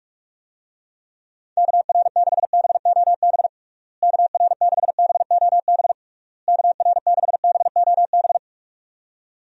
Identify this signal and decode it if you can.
Morse code, heard as a keyed tone, about 46 wpm, 700 Hz. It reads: KR6BOB KR6BOB KR6BOB
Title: USA Call Signs - 6 Characters - Repeated 3x - 45wpm